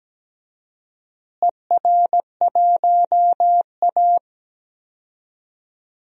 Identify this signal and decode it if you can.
Morse code, heard as a keyed tone, about 17 wpm, 700 Hz. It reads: ER1A
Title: DXCC-100 Call Signs - 4 Characters - Repeated 3x - 17wpm